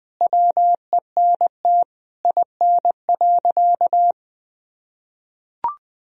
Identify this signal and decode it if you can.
Morse code, heard as a keyed tone, about 20 wpm, 700 Hz. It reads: WENT IN.
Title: Sentences from Top 300 Words and Abbreviations 20wpm